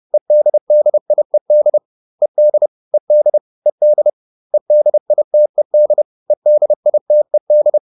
Morse code, heard as a keyed tone, about 30 wpm, 600 Hz. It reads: EDDIED ED ED ED EDITED EDITED